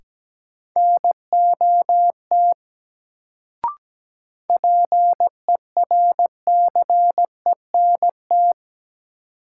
Morse code, heard as a keyed tone, about 17 wpm, 700 Hz. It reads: NOT PERCENT
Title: Sets of 2 Words - New Word Drill - 17wpm